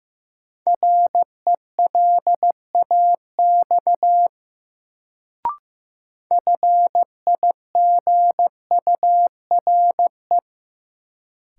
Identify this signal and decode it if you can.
Morse code, heard as a keyed tone, about 16 wpm, 700 Hz. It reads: RELAX FIGURE
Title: Sentences from Top 1400 Words - New Word Drill Alt - 15wpm